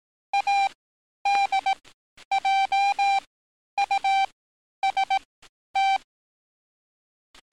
Morse code, heard as a keyed tone, about 18 wpm, spaced as for 10 wpm, 790 Hz. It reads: ADJUST
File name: training / NAVPERS 11049to11060 / NAVPERS 11050 RA RB